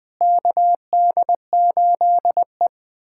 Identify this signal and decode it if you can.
Morse code, heard as a keyed tone, about 21 wpm, 700 Hz. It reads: KD8E
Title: USA Callsign DE Callsign - 4 Characters - 20wpm